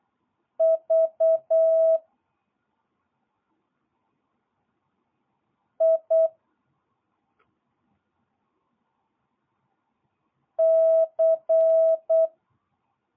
Morse code, heard as a keyed tone, about 8 wpm, 1300 Hz, with noise. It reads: V I C
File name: jh8cbh / 2022cw